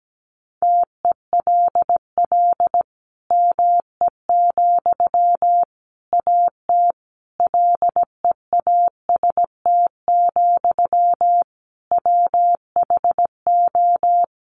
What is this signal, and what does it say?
Morse code, heard as a keyed tone, about 17 wpm, 700 Hz. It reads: TELL ME, AT LEAST, WHO
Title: Sentences from Top 700 Words 17wpm